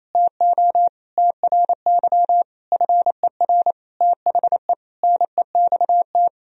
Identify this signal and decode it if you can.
Morse code, heard as a keyed tone, about 28 wpm, 700 Hz. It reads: TO TRY FER THE NEXT
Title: Sentences from Top 400 Words and Abbreviations 28wpm